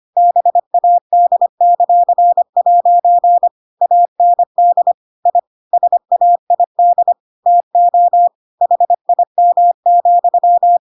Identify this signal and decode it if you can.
Morse code, heard as a keyed tone, about 25 wpm, 700 Hz. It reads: BAD;' AND I SAID TO HIM,
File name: Pinocchio / Pinocchio ch7 25 25 ik1hge